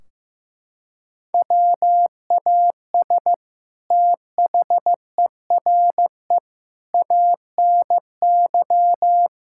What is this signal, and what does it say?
Morse code, heard as a keyed tone, about 15 wpm, 700 Hz. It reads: WAS THERE ANY
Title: Sentences from Top 1600 Words 15wpm